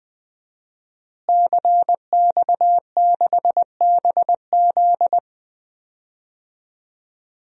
Morse code, heard as a keyed tone, about 21 wpm, 700 Hz. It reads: CX6BZ
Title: DXCC-100 Call Signs - 5 Characters - Repeated 2x - 20wpm